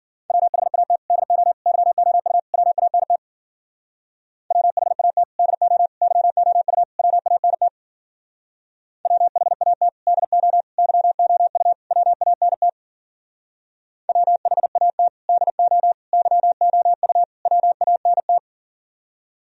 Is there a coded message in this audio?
Yes: Morse code from 0.3 to 18.4 s, heard as a keyed tone, about 51 wpm, 700 Hz. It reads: WHAT DO YOU WANT WHAT DO YOU WANT WHAT DO YOU WANT WHAT DO YOU WANT